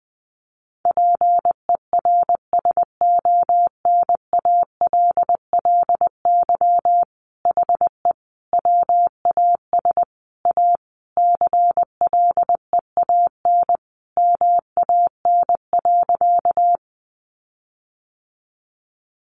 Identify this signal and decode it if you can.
Morse code, heard as a keyed tone, about 20 wpm, 700 Hz. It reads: PERSONALLY HE WAS A CLEAN MAN.
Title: Sentences from Top 1400 Words 20wpm